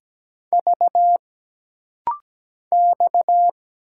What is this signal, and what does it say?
Morse code, heard as a keyed tone, about 17 wpm, 700 Hz. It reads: V X
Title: Single Letter-Number - Rapid-Fire 17wpm